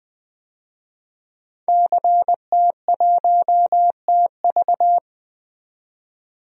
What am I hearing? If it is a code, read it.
Morse code, heard as a keyed tone, about 21 wpm, 700 Hz. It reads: CT1TV
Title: DXCC-100 Call Signs - 5 Characters - 20wpm